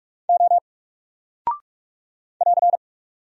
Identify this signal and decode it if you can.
Morse code, heard as a keyed tone, about 44 wpm, 700 Hz. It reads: O <AR>
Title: Single Character (Letter:Number:Punctuation:Prosign) 45wpm